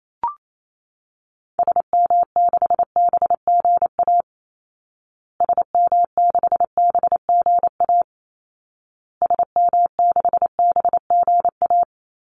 Morse code, heard as a keyed tone, about 28 wpm, 700 Hz. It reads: SM6BGA SM6BGA SM6BGA
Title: Call Signs Repeated 2x - 28wpm